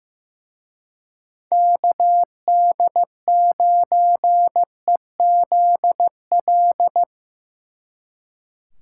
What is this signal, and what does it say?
Morse code, heard as a keyed tone, about 18 wpm, 700 Hz. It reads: KD9EZL
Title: Speed Racing - USA 2x3 Call Signs 15wpm